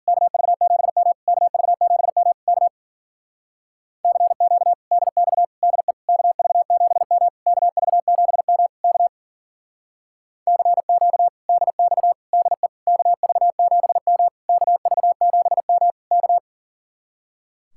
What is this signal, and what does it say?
Morse code, heard as a keyed tone, about 51 wpm, 700 Hz. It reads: KV7M KV7M K CQ DX DE KV7M KV7M K CQ DX DE KV7M KV7M K